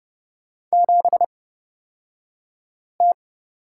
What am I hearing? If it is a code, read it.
Morse code, heard as a keyed tone, about 30 wpm, spaced as for 5 wpm, 700 Hz. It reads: 7T
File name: Lesson 38d - Letter q - Two Characters 30wpm